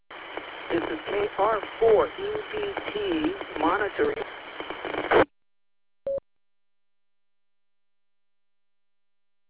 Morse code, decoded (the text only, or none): none